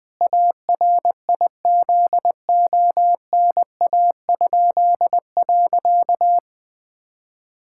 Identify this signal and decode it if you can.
Morse code, heard as a keyed tone, about 21 wpm, 700 Hz. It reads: ARIZONA?.
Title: Pangram Challenge v2 - 20wpm